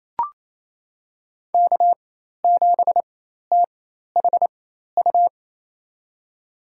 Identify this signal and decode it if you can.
Morse code, heard as a keyed tone, about 27 wpm, spaced as for 13 wpm, 700 Hz. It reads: K7THU
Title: USA Call Signs - 4x Character Spacing - 28wpm